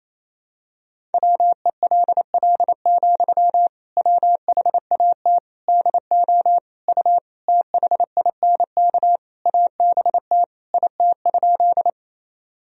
Morse code, heard as a keyed tone, about 28 wpm, 700 Hz. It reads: WELL, WHAT DO U THINK ABT IT?
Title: Sentences from Top 200 Words and Abbreviations 28wpm